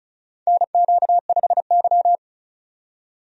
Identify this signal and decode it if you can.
Morse code, heard as a keyed tone, about 36 wpm, 700 Hz. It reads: NQ5Y
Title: USA Call Signs - 4 Characters - 35wpm